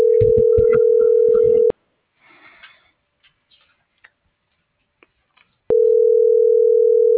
Ringback tone (North America), 440 + 480 Hz.